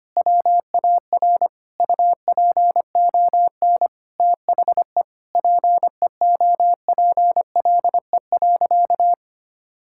Morse code, heard as a keyed tone, about 25 wpm, 700 Hz. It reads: WAR UPON THE PEOPLE.
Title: Sentences from Top 1000 Words - Review Part 6 - 25wpm